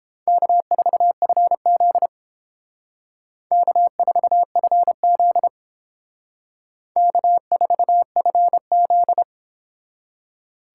Morse code, heard as a keyed tone, about 31 wpm, 700 Hz. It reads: K4FZ K4FZ K4FZ